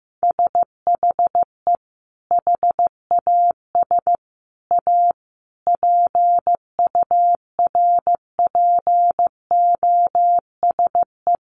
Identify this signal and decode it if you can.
Morse code, heard as a keyed tone, about 15 wpm, 700 Hz. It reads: SHE HAS A PURPOSE
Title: Sentences from Top 800 Words 15wpm